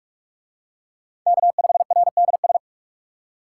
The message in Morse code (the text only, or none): K5RDS